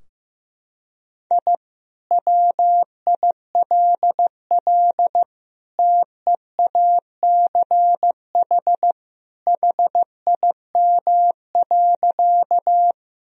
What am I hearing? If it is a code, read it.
Morse code, heard as a keyed tone, about 15 wpm, 700 Hz. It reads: I WILL TEACH HIM.
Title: Sentences from Top 1000 Words - Review Part 2 - 15wpm